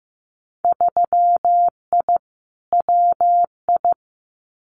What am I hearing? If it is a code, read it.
Morse code, heard as a keyed tone, about 15 wpm, 700 Hz. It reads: 3I WI
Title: Winter Field Day Exchanges 15wpm